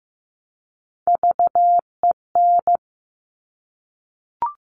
Morse code, heard as a keyed tone, about 15 wpm, 700 Hz. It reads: VEN
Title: Top Three Letter Combinations in Words 15wpm